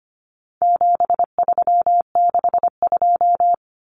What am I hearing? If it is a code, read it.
Morse code, heard as a keyed tone, about 25 wpm, 700 Hz. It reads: 7362
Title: Numbers - 4-Digit 25wpm